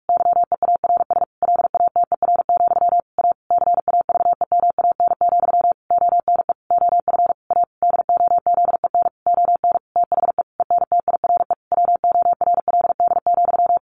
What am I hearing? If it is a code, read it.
Morse code, heard as a keyed tone, about 44 wpm, 700 Hz. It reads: YEARS LATER, A CAVEMAN, ONE OF A DOZEN ON THE ENTIRE WORLD,
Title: A Sound Of Thunder 45wpm